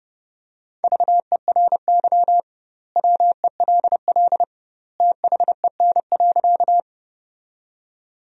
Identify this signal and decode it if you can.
Morse code, heard as a keyed tone, about 30 wpm, 700 Hz. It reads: VERY WELL THEN.
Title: 2x Word-Spacing - Sentences from Top 200 Words - 30wpm